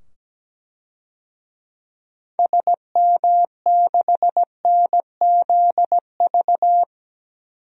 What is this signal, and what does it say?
Morse code, heard as a keyed tone, about 17 wpm, 700 Hz. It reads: SM6NZV
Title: DXCC-100 Call Signs - 6 Characters - 17wpm